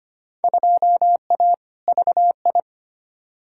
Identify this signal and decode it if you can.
Morse code, heard as a keyed tone, about 25 wpm, 700 Hz. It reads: 2A VI
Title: Field Day Exchanges - 25wpm